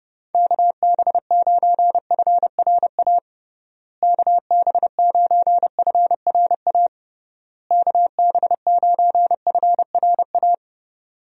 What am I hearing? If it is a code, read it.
Morse code, heard as a keyed tone, about 30 wpm, 700 Hz. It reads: KB9FRA KB9FRA KB9FRA
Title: USA Call Signs - 6 Characters - Repeated 3x - 30wpm